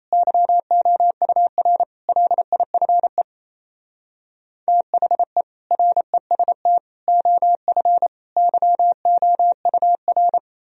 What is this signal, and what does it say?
Morse code, heard as a keyed tone, about 36 wpm, 700 Hz. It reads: YOUR LIFE THE REST OF YOUR